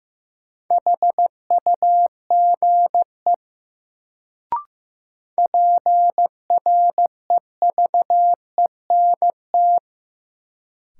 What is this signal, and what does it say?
Morse code, heard as a keyed tone, about 15 wpm, 700 Hz. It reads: HUGE PREVENT
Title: Sentences from Top 1100 Words - New Word Drill - 15wpm